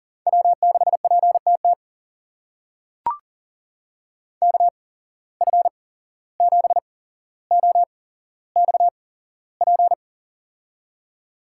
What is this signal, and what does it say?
Morse code, heard as a keyed tone, about 40 wpm, spaced as for 16 wpm, 700 Hz. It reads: W6PTT KF7OXP
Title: USA Call Signs - 8x Character Spacing - 40wpm